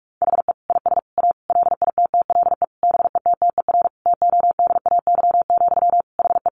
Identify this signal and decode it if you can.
Morse code, heard as a keyed tone, about 44 wpm, 700 Hz. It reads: HE IS A LITTLE BETTER TODAY, HE
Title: Sentences from Top 600 Words 45wpm